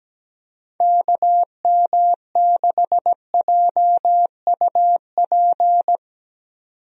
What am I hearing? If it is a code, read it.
Morse code, heard as a keyed tone, about 17 wpm, 700 Hz. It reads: KM6JUP
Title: USA Call Signs - 6 Characters - 17wpm